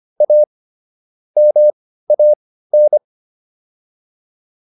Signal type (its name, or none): Morse code